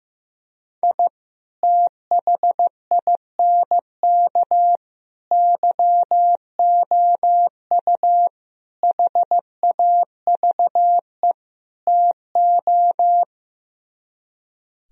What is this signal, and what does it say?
Morse code, heard as a keyed tone, about 15 wpm, 700 Hz. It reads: I THINK YOU HAVE TO
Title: Speed-Racing - Sets of 5 Words 15wpm